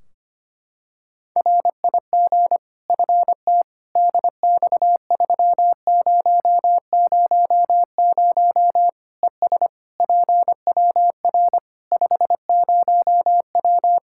Morse code, heard as a keyed tone, about 25 wpm, 700 Hz. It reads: RIG FT DX3000 ES PWR 50W
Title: Practice QSO - Exchange 2 - Info - 25wpm